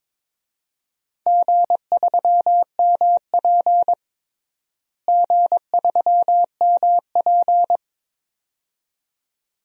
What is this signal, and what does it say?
Morse code, heard as a keyed tone, about 22 wpm, 700 Hz. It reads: G3MP G3MP
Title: DXCC-100 Call Signs - 4 Characters - Repeated 2x - 22wpm